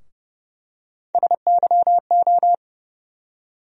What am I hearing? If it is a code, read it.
Morse code, heard as a keyed tone, about 30 wpm, 700 Hz. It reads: SYO
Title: USA Call Sign Postfixes - 3 Letters 30wpm